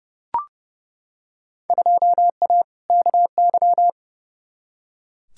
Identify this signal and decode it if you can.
Morse code, heard as a keyed tone, about 30 wpm, 700 Hz. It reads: 2A KY
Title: Field Day Exchanges - 30wpm